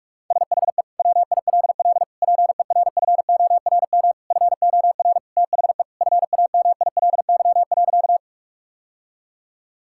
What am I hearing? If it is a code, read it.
Morse code, heard as a keyed tone, about 41 wpm, 700 Hz. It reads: SHE WILL PERFORM FOR THE FAMILY.
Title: Sentences from Top 1000 Words - Review Part 5 - 45wpm